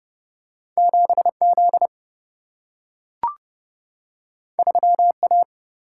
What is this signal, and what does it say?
Morse code, heard as a keyed tone, about 30 wpm, 700 Hz. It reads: 7 Z 3 A
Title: Single-Letter Single-Number 30wpm